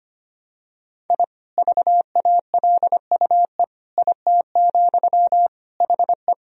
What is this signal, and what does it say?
Morse code, heard as a keyed tone, about 25 wpm, 700 Hz. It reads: I VALUE IT, HE
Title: Sentences from Top 1000 Words - Review Part 3 - 25wpm